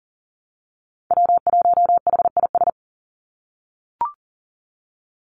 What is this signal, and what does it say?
Morse code, heard as a keyed tone, about 40 wpm, 700 Hz. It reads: W1HIS
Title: Call Signs - 40wpm